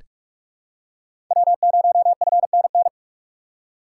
Morse code, heard as a keyed tone, about 42 wpm, 700 Hz. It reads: W0FNN
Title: USA Call Signs - 5 Characters - 2x Character Spacing - 45wpm